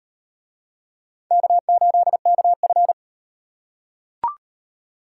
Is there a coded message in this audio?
Yes: Morse code from 1.3 to 2.9 s, heard as a keyed tone, about 38 wpm, 700 Hz. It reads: K8KF